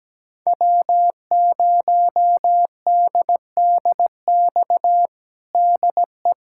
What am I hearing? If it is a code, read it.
Morse code, heard as a keyed tone, about 17 wpm, 700 Hz. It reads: W0DDX DE
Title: USA Callsign DE Callsign - 5 Characters - 17wpm